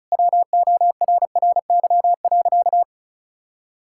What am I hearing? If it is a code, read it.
Morse code, heard as a keyed tone, about 36 wpm, 700 Hz. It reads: WORRY.
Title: Pangram Challenge v2 - 35wpm